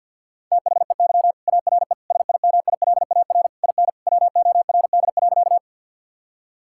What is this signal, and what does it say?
Morse code, heard as a keyed tone, about 50 wpm, 700 Hz. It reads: THEY ARE SIMILAR IN WORD.
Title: Sentences from Top 1000 Words - Review Part 5 - 50wpm